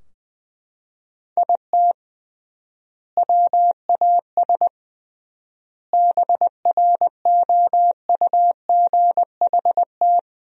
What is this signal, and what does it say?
Morse code, heard as a keyed tone, about 20 wpm, 700 Hz. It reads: IT WAS BROUGHT
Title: 3x Word-Spacing - Sentences from Top 500 Words - 20wpm